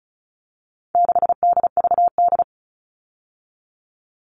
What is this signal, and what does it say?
Morse code, heard as a keyed tone, about 34 wpm, 700 Hz. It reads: 6DVD
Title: All Cut Numbers - 4-Digit 35wpm